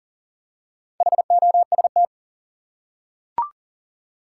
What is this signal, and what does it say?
Morse code, heard as a keyed tone, about 40 wpm, 700 Hz. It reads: HOST
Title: Four Letter Words - 40wpm